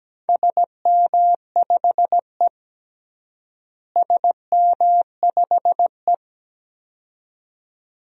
Morse code, heard as a keyed tone, about 17 wpm, 700 Hz. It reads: SM5E SM5E
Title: DXCC-100 Call Signs - 4 Characters - Repeated 2x - 17wpm